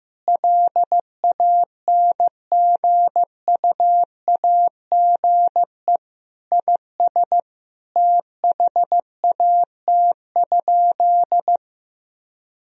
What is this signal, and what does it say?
Morse code, heard as a keyed tone, about 15 wpm, 700 Hz. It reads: LANGUAGE IS THAT?
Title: Sentences from Top 1000 Words - Review Part 3 - 15wpm